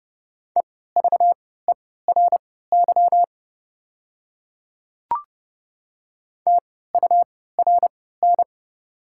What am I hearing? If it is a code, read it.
Morse code, heard as a keyed tone, about 29 wpm, spaced as for 17 wpm, 690 Hz. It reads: EVERY TURN
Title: 3x Character Spacing - Top 500 Words - 30wpm